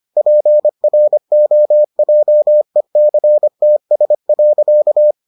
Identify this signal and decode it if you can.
Morse code, heard as a keyed tone, about 25 wpm, 600 Hz. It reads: PROJECTS.